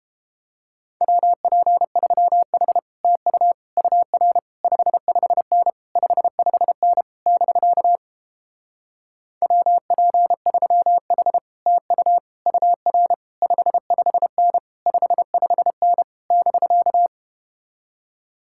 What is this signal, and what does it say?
Morse code, heard as a keyed tone, about 31 wpm, 700 Hz. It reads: WP3H TU UR 55N 55N <BK> WP3H TU UR 55N 55N <BK>